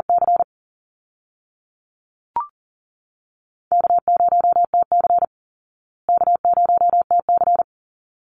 Morse code, heard as a keyed tone, about 40 wpm, 700 Hz. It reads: C K0TC K0TC
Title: Easy Call Signs Repeated 2x - 40wpm